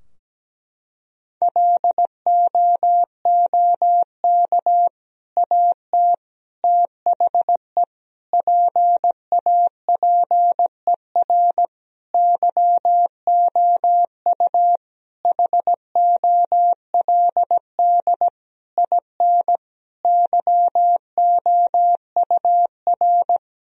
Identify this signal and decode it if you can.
Morse code, heard as a keyed tone, about 17 wpm, 700 Hz. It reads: LOOK AT THE PAPER YOU HOLD IN YOUR
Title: Sentences from Top 1000 Words - Review Part 2 - 17wpm